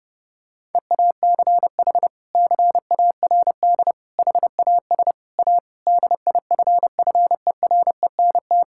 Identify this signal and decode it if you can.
Morse code, heard as a keyed tone, about 30 wpm, 700 Hz. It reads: EACH CARD HAS A DIFFERENT
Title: Sentences from Top 1000 Words - Review Part 6 - 30wpm